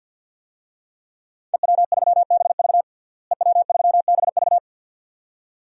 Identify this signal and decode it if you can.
Morse code, heard as a keyed tone, about 50 wpm, 700 Hz. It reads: EW3BV EW3BV
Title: DXCC-100 Call Signs - 5 Characters - Repeated 2x - 50wpm